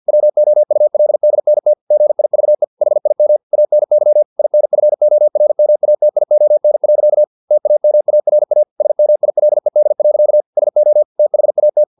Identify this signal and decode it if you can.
Morse code, heard as a keyed tone, about 50 wpm, 600 Hz. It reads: WOULDNT GIVE HIM ANY INFORMATION. TAMARA SMILED. SO THAT
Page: Never 49wpm